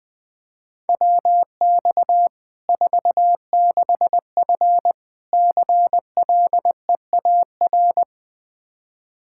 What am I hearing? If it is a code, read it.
Morse code, heard as a keyed tone, about 20 wpm, 700 Hz. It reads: WX 46F CLEAR
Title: Practice QSO - Exchange 2 - Info - 20wpm